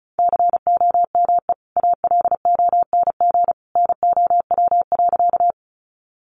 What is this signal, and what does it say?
Morse code, heard as a keyed tone, about 34 wpm, 700 Hz. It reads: COME ALONG NOW.
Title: Sentences from Top 700 Words 35wpm